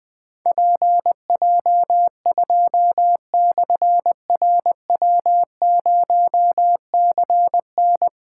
Morse code, heard as a keyed tone, about 20 wpm, 700 Hz. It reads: PJ2/RW0CN
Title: Difficult Call Signs 20wpm